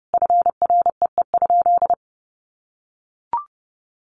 Morse code, heard as a keyed tone, about 30 wpm, 700 Hz. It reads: FREE?
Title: Sentences from Top 1100 Words 30wpm